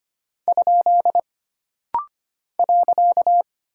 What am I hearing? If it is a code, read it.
Morse code, heard as a keyed tone, about 25 wpm, 700 Hz. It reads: ? .
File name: Single Punctuation-Prosign - Rapid-Fire - 25wpm